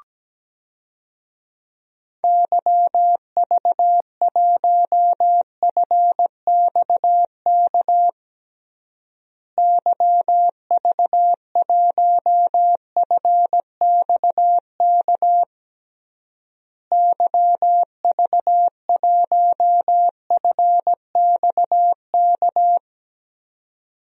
Morse code, heard as a keyed tone, about 17 wpm, 700 Hz. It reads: YV1FXK YV1FXK YV1FXK
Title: DXCC-100 Call Signs - 6 Characters - Repeated 3x - 17wpm